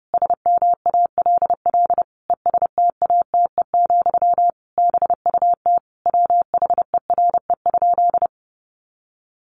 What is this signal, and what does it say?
Morse code, heard as a keyed tone, about 30 wpm, 700 Hz. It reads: SMALL ESTATE, BUT WHERE?
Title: Sentences from Top 1200 Words 30wpm